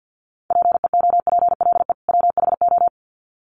Morse code, heard as a keyed tone, about 50 wpm, 700 Hz. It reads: PEOPLE WHO
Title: Sets of 2 Words 50wpm